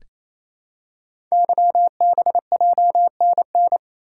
Morse code, heard as a keyed tone, about 28 wpm, 700 Hz. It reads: YBJNN
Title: Code Groups - 5 Letters - 28wpm